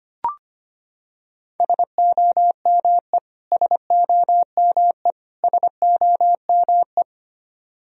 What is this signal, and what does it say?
Morse code, heard as a keyed tone, about 25 wpm, 700 Hz. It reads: SOME SOME SOME
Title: Top 100 Words Repeated 25wpm